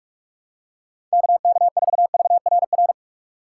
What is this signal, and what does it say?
Morse code, heard as a keyed tone, about 46 wpm, 700 Hz. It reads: KK4VRR
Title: USA Call Signs - 3x Character Spacing - 45wpm